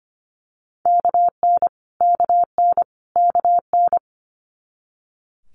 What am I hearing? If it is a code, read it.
Morse code, heard as a keyed tone, about 25 wpm, 700 Hz. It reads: KN KN KN
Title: Multi-letter Phonemes Repeated 3x 25wpm